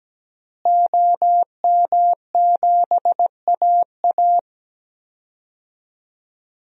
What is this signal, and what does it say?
Morse code, heard as a keyed tone, about 17 wpm, 700 Hz. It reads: OM7AA
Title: DXCC-100 Call Signs - 5 Characters - Repeated 3x - 17wpm